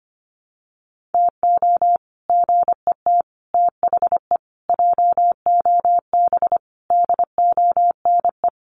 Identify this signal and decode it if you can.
Morse code, heard as a keyed tone, about 25 wpm, 700 Hz. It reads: TO GET THE JOB DONE
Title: Sets of 5 Words Encore2 25wpm